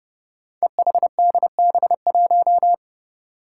Morse code, heard as a keyed tone, about 32 wpm, 700 Hz. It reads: EHDB1
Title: Five Challenging Letter-Number Combinations 30wpm